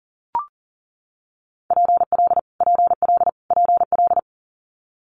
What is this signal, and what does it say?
Morse code, heard as a keyed tone, about 39 wpm, 690 Hz. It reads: PL PL PL
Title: Multi-letter Phonemes Repeated 3x 40wpm